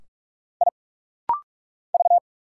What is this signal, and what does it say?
Morse code, heard as a keyed tone, about 45 wpm, 700 Hz. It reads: I V
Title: Single Letter-Number - Rapid-Fire 45wpm